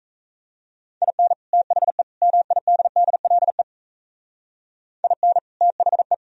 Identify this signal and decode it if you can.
Morse code, heard as a keyed tone, about 36 wpm, 700 Hz. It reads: IN THE MIDDLE IN THE